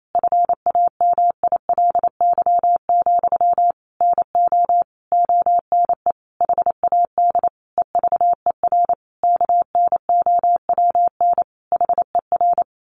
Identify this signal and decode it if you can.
Morse code, heard as a keyed tone, about 28 wpm, 700 Hz. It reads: FAMILY, NO ONE HAD EVER KNOWN HER
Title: Sentences from Top 1200 Words 28wpm